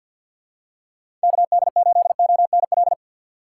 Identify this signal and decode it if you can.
Morse code, heard as a keyed tone, about 51 wpm, 700 Hz. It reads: KD8ONL
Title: USA Call Signs - 6 Characters - 3x Character Spacing - 50wpm